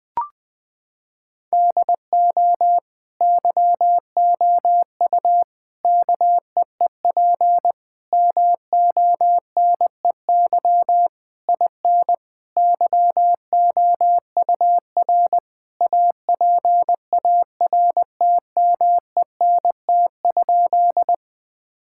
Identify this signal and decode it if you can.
Morse code, heard as a keyed tone, about 21 wpm, 700 Hz. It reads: DO YOU KEEP MONEY IN YOUR APARTMENT?
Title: Sentences from Top 1000 Words - Review Part 6 - 20wpm